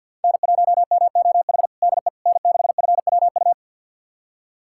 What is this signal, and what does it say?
Morse code, heard as a keyed tone, about 49 wpm, 700 Hz. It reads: N1MOH DE N6FWU
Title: USA Callsign DE Callsign - 5 Characters - 50wpm